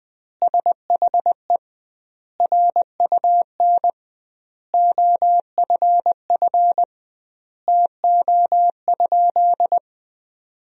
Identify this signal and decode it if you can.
Morse code, heard as a keyed tone, about 20 wpm, 700 Hz. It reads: SHE RUN OFF TO?
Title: 2x Word-Spacing - Sentences from Top 300 Words - 20wpm